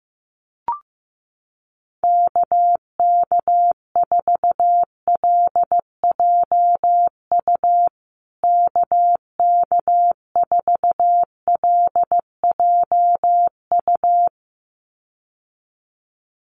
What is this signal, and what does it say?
Morse code, heard as a keyed tone, about 15 wpm, 700 Hz. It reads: KK4LJU KK4LJU
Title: USA Call Signs Repeated 2x 15wpm